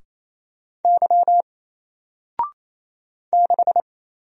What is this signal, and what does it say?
Morse code, heard as a keyed tone, about 27 wpm, 700 Hz. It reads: Y 6
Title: Single Letter-Number Alt 28wpm